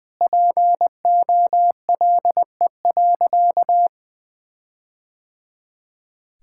Morse code, heard as a keyed tone, about 20 wpm, 700 Hz. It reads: POLE.
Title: Sentences with 4-Word Phrases - 20wpm